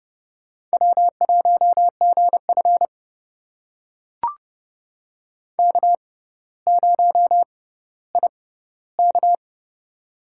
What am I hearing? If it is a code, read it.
Morse code, heard as a keyed tone, about 30 wpm, spaced as for 15 wpm, 700 Hz. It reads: W1GF K0IK
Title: USA Call Signs - 4 Characters - 6x Character Spacing - 30wpm